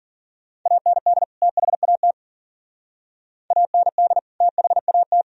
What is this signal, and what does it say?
Morse code, heard as a keyed tone, about 53 wpm, 700 Hz. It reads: AND THAT AND THAT